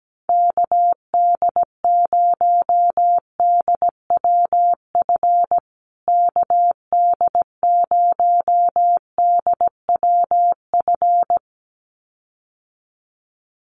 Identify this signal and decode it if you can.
Morse code, heard as a keyed tone, about 17 wpm, 700 Hz. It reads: KD0DWF KD0DWF
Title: USA Call Signs Repeated 2x 17wpm